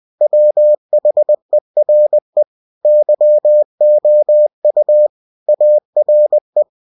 Morse code, heard as a keyed tone, about 20 wpm, 600 Hz. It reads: WHERE YOU ARE
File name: cw-books / winnie-the-pooh / winnie-20wpm-page-0114